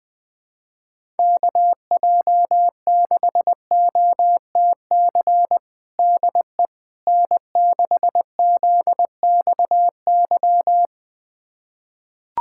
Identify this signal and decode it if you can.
Morse code, heard as a keyed tone, about 20 wpm, 700 Hz. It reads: KJ6OTC DE N6ZXY
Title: USA Callsign DE Callsign 20wpm